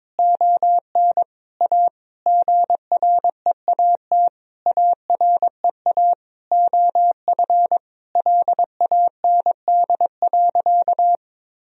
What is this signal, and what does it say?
Morse code, heard as a keyed tone, about 22 wpm, 700 Hz. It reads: ON A GREAT AREA OF LAND.
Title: Sentences from Top 1000 Words - Review Part 2 - 22wpm